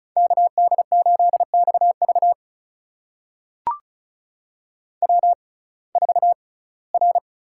Morse code, heard as a keyed tone, about 36 wpm, spaced as for 21 wpm, 700 Hz. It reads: KD8XV W4R
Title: USA Call Signs - 5 Characters - 6x Character Spacing - 35wpm